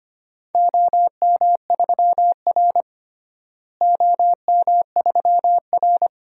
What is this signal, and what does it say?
Morse code, heard as a keyed tone, about 25 wpm, 700 Hz. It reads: OM3R OM3R
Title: DXCC-100 Call Signs - 4 Characters - Repeated 2x - 25wpm